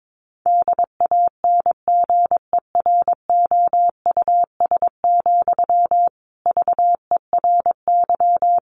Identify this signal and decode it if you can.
Morse code, heard as a keyed tone, about 22 wpm, 700 Hz. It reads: DANGEROUS, VERY
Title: Sentences from Top 1200 Words 22wpm